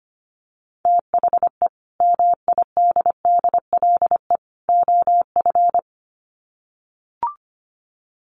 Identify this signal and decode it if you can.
Morse code, heard as a keyed tone, about 25 wpm, 700 Hz. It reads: THE MIDDLE OF
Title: Sets of 3 Words 25wpm